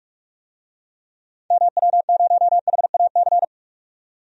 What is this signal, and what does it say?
Morse code, heard as a keyed tone, about 43 wpm, 700 Hz. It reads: MW0HAC
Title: DXCC-100 Call Signs - 6 Characters - 45wpm